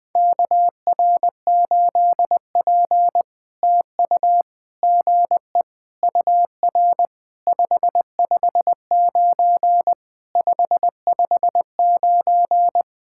Morse code, heard as a keyed tone, about 20 wpm, 700 Hz. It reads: KR8P TU GE UR 559 559
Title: POTA QSO Practice - 20wpm